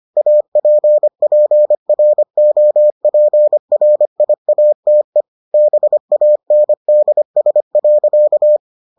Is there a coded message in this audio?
Yes: Morse code from 0.2 to 8.6 s, heard as a keyed tone, about 25 wpm, 600 Hz. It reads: APPROPRIATE BANDS.